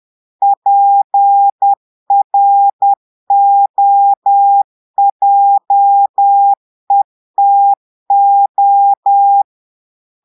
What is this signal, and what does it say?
Morse code, heard as a keyed tone, about 10 wpm, 800 Hz. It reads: PROJETO
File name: portugais texte en clair 2 vitesse 10 mots minutes